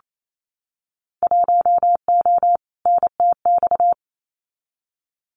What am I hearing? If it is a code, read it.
Morse code, heard as a keyed tone, about 28 wpm, 700 Hz. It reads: 1O NTX
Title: Winter Field Day Exchanges Encore 28wpm